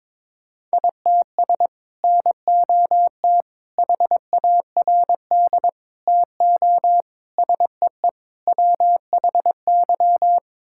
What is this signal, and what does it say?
Morse code, heard as a keyed tone, about 22 wpm, 700 Hz. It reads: ITS NOT HARD TO SEE WHY
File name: practice / 22wpm / Sets of 6 Words - 22wpm